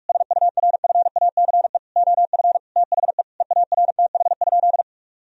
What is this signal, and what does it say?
Morse code, heard as a keyed tone, about 41 wpm, 700 Hz. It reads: SURFACE OF THE EARTH?
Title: Sentences from Top 1000 Words - Review Part 3 - 45wpm